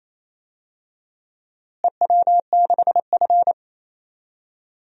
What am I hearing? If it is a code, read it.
Morse code, heard as a keyed tone, about 28 wpm, 700 Hz. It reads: EW6F
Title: DXCC-100 Call Signs - 4 Characters - Repeated 2x - 28wpm